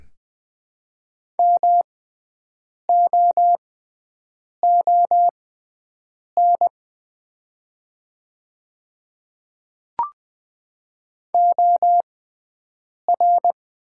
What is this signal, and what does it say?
Morse code, heard as a keyed tone, about 20 wpm, spaced as for 7 wpm, 700 Hz. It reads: MOON OR
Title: 6x Character Spacing - Top 500 Words - 20wpm